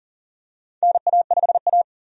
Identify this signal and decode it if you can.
Morse code, heard as a keyed tone, about 40 wpm, 700 Hz. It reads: NA5A
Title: USA Call Signs - 4 Characters - 40wpm